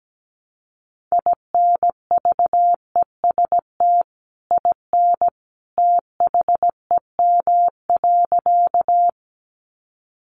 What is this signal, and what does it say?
Morse code, heard as a keyed tone, about 17 wpm, 700 Hz. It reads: INVEST IN THEM.
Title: Sentences from Top 1100 Words 17wpm